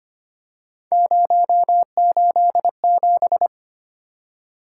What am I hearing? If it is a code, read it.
Morse code, heard as a keyed tone, about 25 wpm, 700 Hz. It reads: 087
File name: Three Letter-Number Combinations - 25wpm